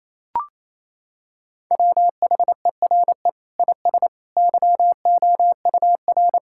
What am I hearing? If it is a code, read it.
Morse code, heard as a keyed tone, about 28 wpm, 700 Hz. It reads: WHERE IS YOUR
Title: Sentences from Top 1600 Words 28wpm